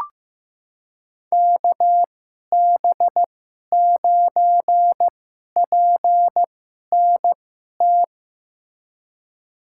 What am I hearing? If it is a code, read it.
Morse code, heard as a keyed tone, about 15 wpm, spaced as for 11 wpm, 700 Hz. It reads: KB9PNT
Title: USA Call Signs - 6 Characters - 2x Character Spacing - 15wpm